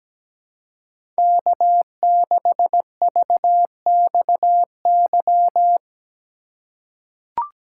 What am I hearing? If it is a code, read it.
Morse code, heard as a keyed tone, about 17 wpm, 700 Hz. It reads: K6VXY
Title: USA Call Signs - 5 Characters - 4x Character Spacing - 17wpm